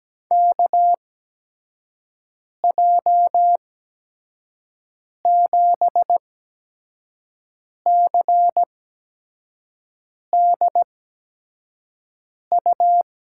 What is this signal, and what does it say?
Morse code, heard as a keyed tone, about 17 wpm, spaced as for 5 wpm, 700 Hz. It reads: KJ7CDU
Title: USA Call Signs - 6 Characters - 8x Character Spacing - 17wpm